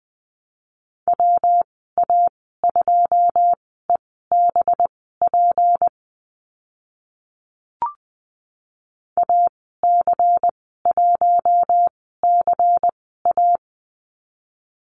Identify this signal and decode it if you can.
Morse code, heard as a keyed tone, about 19 wpm, spaced as for 14 wpm, 700 Hz. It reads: WA2EBP AC1CA
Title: 186 Lesson 30g - Letter b - Call Signs 20wpm